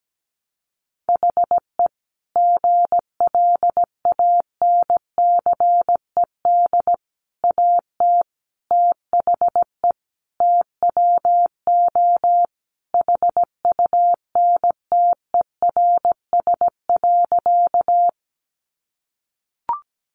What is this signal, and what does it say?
Morse code, heard as a keyed tone, about 17 wpm, 700 Hz. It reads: HE GLANCED AT THE TWO HUNTERS.
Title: A Sound Of Thunder 17wpm